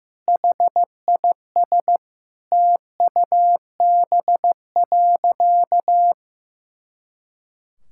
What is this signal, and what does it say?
Morse code, heard as a keyed tone, about 14 wpm, 700 Hz. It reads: HIS TUB.
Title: Pangram Challenge v2 - 15wpm